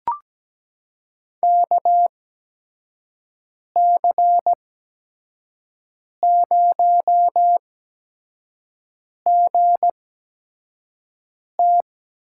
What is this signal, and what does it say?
Morse code, heard as a keyed tone, about 17 wpm, spaced as for 5 wpm, 700 Hz. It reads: KC0GT
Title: USA Call Signs - 6 Characters - 8x Character Spacing - 17wpm